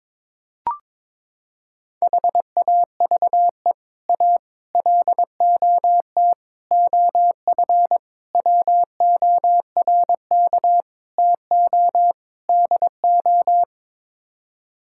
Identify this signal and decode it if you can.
Morse code, heard as a keyed tone, about 22 wpm, 700 Hz. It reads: HAVE A LOT OF WORK TO DO
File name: practice / 22wpm / Sets of 7 Words - 22wpm